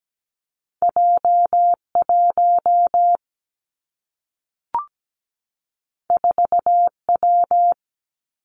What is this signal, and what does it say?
Morse code, heard as a keyed tone, about 17 wpm, 700 Hz. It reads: J1 4W
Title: Two Challenging Letter-Number Combinations 17wpm